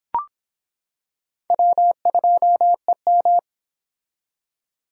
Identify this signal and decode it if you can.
Morse code, heard as a keyed tone, about 24 wpm, 700 Hz. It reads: W2EM